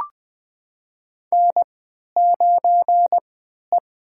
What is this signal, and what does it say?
Morse code, heard as a keyed tone, about 20 wpm, spaced as for 11 wpm, 700 Hz. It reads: N9E
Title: USA Call Signs - 3x Character Spacing - 20wpm